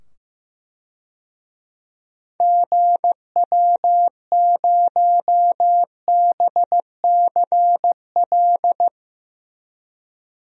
Morse code, heard as a keyed tone, about 15 wpm, 700 Hz. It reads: GW0BCL
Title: DXCC-100 Call Signs - 6 Characters - 15wpm